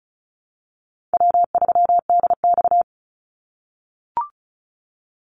Morse code, heard as a keyed tone, about 34 wpm, 700 Hz. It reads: W3DX
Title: NA SOTA Activator Callsigns 35wpm